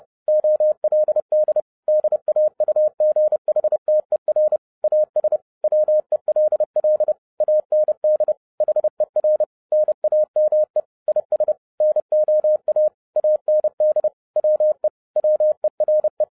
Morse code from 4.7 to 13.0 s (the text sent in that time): AS WELL AND HER NAME IS NOA